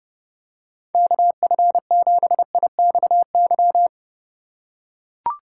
Morse code, heard as a keyed tone, about 29 wpm, 700 Hz. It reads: KF7IXY